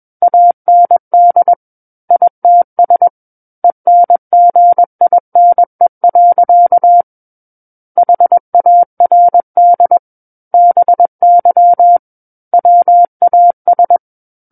Morse code, heard as a keyed tone, about 21 wpm, 700 Hz. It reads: AND ITS ENGINE. HARD BY WAS
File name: Book One / 12-Chapter 12 - 21 WPM